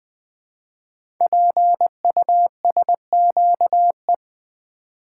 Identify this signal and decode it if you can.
Morse code, heard as a keyed tone, about 20 wpm, 700 Hz. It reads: PUSQE
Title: Code Groups - 5 Letters - 20wpm